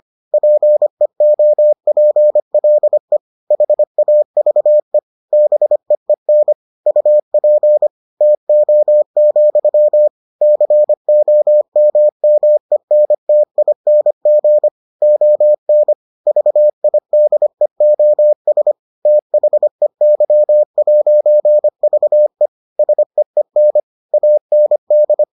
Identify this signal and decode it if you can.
Morse code, heard as a keyed tone, about 25 wpm, 600 Hz. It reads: PEOPLE HAVE BEEN UP TO, COMMENTING ON VIDEOS THEY'VE SEEN AND